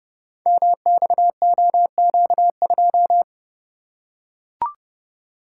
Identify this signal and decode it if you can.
Morse code, heard as a keyed tone, about 28 wpm, 700 Hz. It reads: MXOQ2
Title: Code Groups - 5 Letter:Numbers - 30wpm